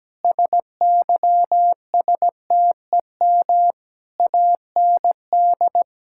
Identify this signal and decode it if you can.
Morse code, heard as a keyed tone, about 17 wpm, 700 Hz. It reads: SYSTEM AND
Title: Sentences from Top 1000 Words - Review Part 5 - 17wpm